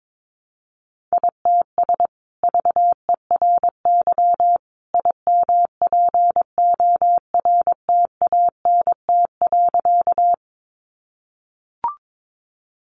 Morse code, heard as a keyed tone, about 22 wpm, 700 Hz. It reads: ITS VERY IMPORTANT.
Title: Sentences from Top 600 Words 22wpm